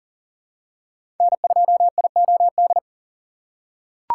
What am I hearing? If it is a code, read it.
Morse code, heard as a keyed tone, about 37 wpm, 690 Hz. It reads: N2IOD
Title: USA Call Signs - 5 Characters - 3x Character Spacing - 40wpm